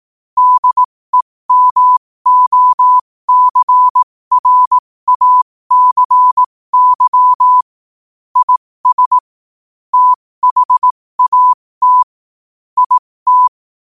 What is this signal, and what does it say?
Morse code, heard as a keyed tone, about 18 wpm, spaced as for 15 wpm, 1000 Hz. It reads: DEMOCRACY IS THAT IT
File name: qotdfiles / 15wpm